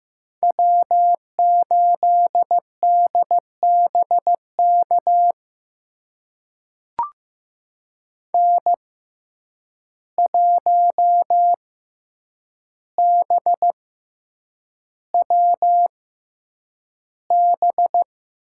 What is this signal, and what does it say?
Morse code, heard as a keyed tone, about 15 wpm, spaced as for 8 wpm, 700 Hz. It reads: W8DBK N1BWB
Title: USA Call Signs - 5 Characters - 6x Character Spacing - 15wpm